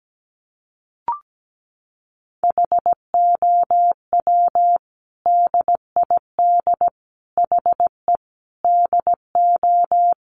Morse code, heard as a keyed tone, about 17 wpm, 700 Hz. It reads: HOW DID HE DO
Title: Sentences from Top 100 Words 17wpm